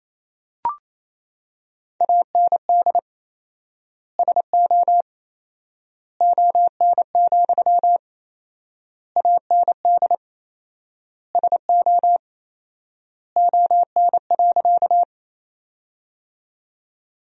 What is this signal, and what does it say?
Morse code, heard as a keyed tone, about 28 wpm, 700 Hz. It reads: AND SO ON, AND SO ON.
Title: 4x Word-Spacing - Sentences from Top 100 Words - 28wpm